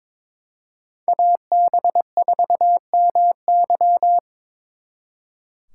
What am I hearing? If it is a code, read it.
Morse code, heard as a keyed tone, about 26 wpm, 700 Hz. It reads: AB4MY
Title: Speed Racing - USA 2x2 Call Signs 22wpm